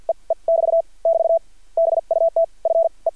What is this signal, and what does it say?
Morse code, heard as a keyed tone, about 50 wpm, spaced as for 38 wpm, 600 Hz. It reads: E E <BT> <BT> BUT FE
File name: archived / CRSnet